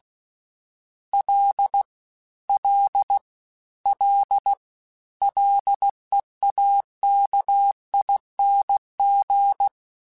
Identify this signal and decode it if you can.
Morse code, heard as a keyed tone, about 16 wpm, 800 Hz. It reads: L L L LEAKING
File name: Combinations / Combinations-mp3s / Combinations@15